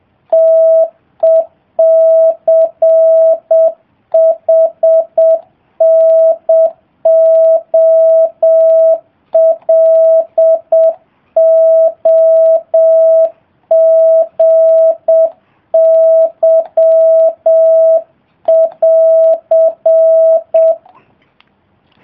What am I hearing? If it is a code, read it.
Morse code, heard as a keyed tone, about 7 wpm, 650 Hz. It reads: TECHNOLOGY<AR>